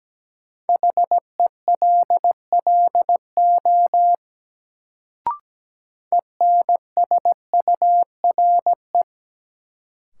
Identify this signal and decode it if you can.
Morse code, heard as a keyed tone, about 17 wpm, 700 Hz. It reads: HELLO ENSURE
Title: Sentences from Top 1100 Words - New Word Drill Alt - 17wpm